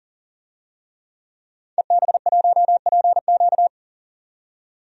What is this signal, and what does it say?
Morse code, heard as a keyed tone, about 40 wpm, 700 Hz. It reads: EB1PQ
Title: DXCC-100 Call Signs - 5 Characters - 40wpm